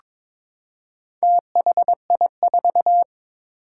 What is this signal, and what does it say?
Morse code, heard as a keyed tone, about 22 wpm, 700 Hz. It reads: THI4
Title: Four Challenging Letter-Number Combinations 22wpm